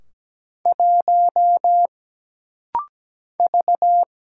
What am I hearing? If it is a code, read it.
Morse code, heard as a keyed tone, about 17 wpm, 700 Hz. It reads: 1 V
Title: Single Letter-Number - Rapid-Fire 17wpm